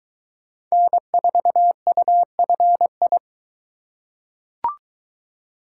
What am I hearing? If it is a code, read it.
Morse code, heard as a keyed tone, about 20 wpm, 700 Hz. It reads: N4UFI